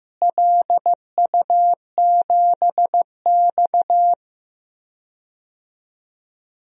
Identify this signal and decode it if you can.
Morse code, heard as a keyed tone, about 15 wpm, 700 Hz. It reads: LU7X
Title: DXCC-100 Call Signs - 4 Characters - Repeated 3x - 15wpm